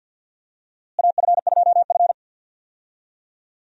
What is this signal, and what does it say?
Morse code, heard as a keyed tone, about 52 wpm, 700 Hz. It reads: AF2F
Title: USA Call Signs - 4 Characters - Repeated 3x - 50wpm